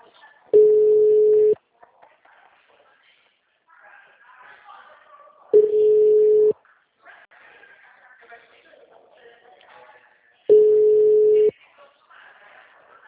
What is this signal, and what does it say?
Ringback tone (Europe, 425 Hz), with speech.